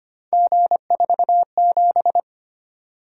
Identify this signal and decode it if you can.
Morse code, heard as a keyed tone, about 25 wpm, 700 Hz. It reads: G47
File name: Three Letter-Number Combinations - 25wpm